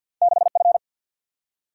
Morse code, heard as a keyed tone, about 50 wpm, 700 Hz. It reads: 6 F
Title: Single-Letter Single-Number 50wpm